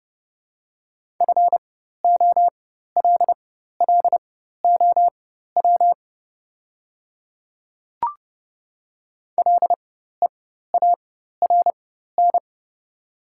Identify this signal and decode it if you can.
Morse code, heard as a keyed tone, about 30 wpm, spaced as for 14 wpm, 700 Hz. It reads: FOLLOW LEARN
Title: 4x Character Spacing - Top 300 Words - 30wpm